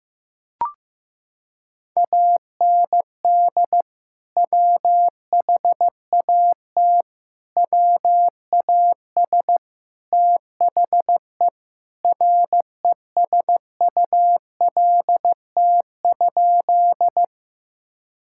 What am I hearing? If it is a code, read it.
Morse code, heard as a keyed tone, about 15 wpm, 700 Hz. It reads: AND WHAT WAS THE RESULT?
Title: Sentences from Top 1000 Words - Review Part 5 - 15wpm